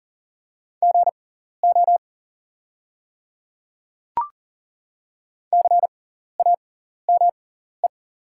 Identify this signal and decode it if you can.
Morse code, heard as a keyed tone, about 40 wpm, spaced as for 14 wpm, 700 Hz. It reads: GO CAME
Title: 6x Character Spacing - Top 200 Words - 40wpm